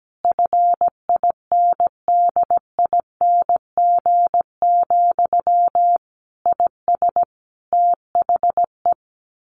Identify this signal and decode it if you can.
Morse code, heard as a keyed tone, about 17 wpm, 700 Hz. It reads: FINDING, IS THE
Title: Sentences from Top 1100 Words 17wpm